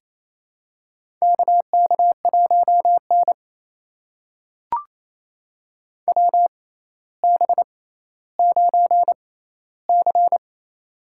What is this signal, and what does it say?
Morse code, heard as a keyed tone, about 28 wpm, spaced as for 14 wpm, 700 Hz. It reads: KK1N WB9C
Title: USA Call Signs - 6x Character Spacing - 28wpm